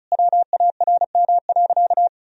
Morse code, heard as a keyed tone, about 34 wpm, 700 Hz. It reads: WARM.
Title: Sentences from Top 1000 Words - Review Part 2 - 35wpm